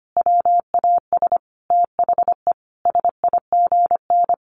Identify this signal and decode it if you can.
Morse code, heard as a keyed tone, about 25 wpm, 700 Hz. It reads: WAS THE SIGN
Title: A Sound Of Thunder 25wpm